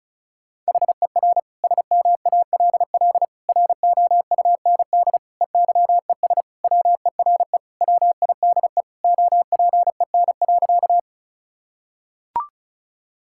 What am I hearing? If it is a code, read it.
Morse code, heard as a keyed tone, about 36 wpm, 700 Hz. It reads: HER SMALL ROUND EYES WERE WIDE OPEN.
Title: Sentences from Top 1000 Words - Review Part 6 - 35wpm